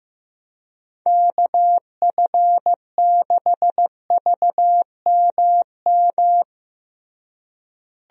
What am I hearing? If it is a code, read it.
Morse code, heard as a keyed tone, about 15 wpm, 700 Hz. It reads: KF6VMM
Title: USA Call Signs - 6 Characters - Repeated 3x - 15wpm